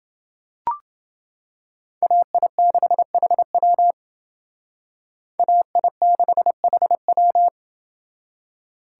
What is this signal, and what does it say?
Morse code, heard as a keyed tone, about 28 wpm, 700 Hz. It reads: AI6HW AI6HW